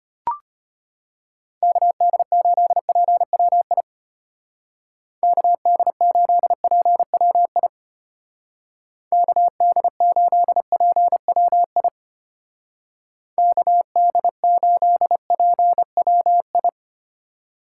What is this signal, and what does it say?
Morse code, heard as a keyed tone, about 32 wpm, 700 Hz. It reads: KD8PWI KD8PWI KD8PWI KD8PWI